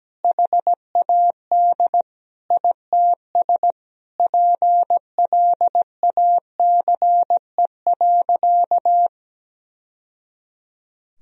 Morse code, heard as a keyed tone, about 17 wpm, 700 Hz. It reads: HAD ITS PLACE.
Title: Sentences from Top 1000 Words - Review Part 1 - 17wpm